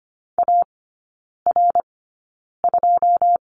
Morse code, heard as a keyed tone, about 25 wpm, spaced as for 9 wpm, 700 Hz. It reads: AR2
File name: course / 25:10wpm / 093 Lesson 16e - Letter c - Three Characters 25wpm